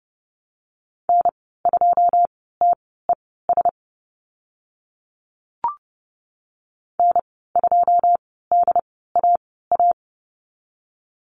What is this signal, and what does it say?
Morse code, heard as a keyed tone, about 30 wpm, spaced as for 17 wpm, 700 Hz. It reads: N2TES N2DAA
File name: Lesson 14g - Number 2 - Call Signs 30wpm